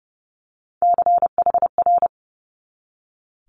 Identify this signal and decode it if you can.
Morse code, heard as a keyed tone, about 30 wpm, 700 Hz. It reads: CHR
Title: Multi-letter Phonemes 30wpm